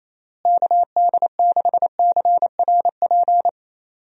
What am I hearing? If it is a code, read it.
Morse code, heard as a keyed tone, about 27 wpm, 700 Hz. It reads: KD6CRP
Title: USA Call Signs - 6 Characters - Repeated 3x - 28wpm